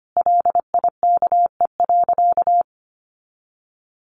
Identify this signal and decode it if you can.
Morse code, heard as a keyed tone, about 25 wpm, 700 Hz. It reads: LIKE.
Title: Sentences from Top 1400 Words 25wpm